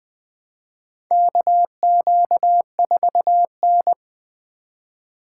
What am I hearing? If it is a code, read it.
Morse code, heard as a keyed tone, about 20 wpm, 700 Hz. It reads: KQ4N
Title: USA Call Signs - 4 Characters - 2x Character Spacing - 20wpm